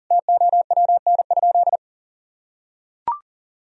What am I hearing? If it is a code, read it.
Morse code, heard as a keyed tone, about 40 wpm, 700 Hz. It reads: TOWN?
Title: Sentences from Top 1000 Words - Review Part 3 - 40wpm